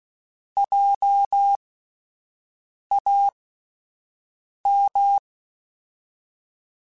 Morse code, heard as a keyed tone, about 16 wpm, spaced as for 5 wpm, 800 Hz. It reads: JAM